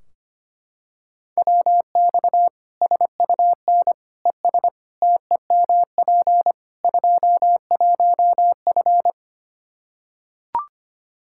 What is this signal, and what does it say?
Morse code, heard as a keyed tone, about 25 wpm, 700 Hz. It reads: WX SUN ES TEMP 21F
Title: Practice QSO - Exchange 2 - Info - 25wpm